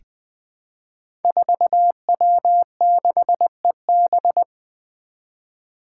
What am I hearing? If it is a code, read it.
Morse code, heard as a keyed tone, about 20 wpm, 700 Hz. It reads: 4W6EB
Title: Five Challenging Letter-Number Combinations 20wpm